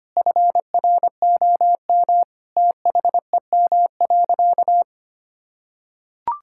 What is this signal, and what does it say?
Morse code, heard as a keyed tone, about 25 wpm, 700 Hz. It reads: FROM THEM.
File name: practice / 25wpm / Difficult Sentences with Hints - 25wpm